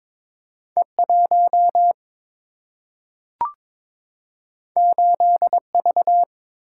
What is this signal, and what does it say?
Morse code, heard as a keyed tone, about 22 wpm, 700 Hz. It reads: E 1 8 V
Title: Single-Letter Single-Number 22wpm